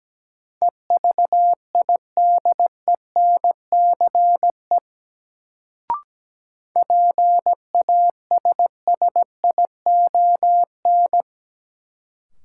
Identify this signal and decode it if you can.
Morse code, heard as a keyed tone, about 17 wpm, 700 Hz. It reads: EVIDENCE PASSION
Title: Sentences from Top 1300 Words - New Word Drill Alt - 17wpm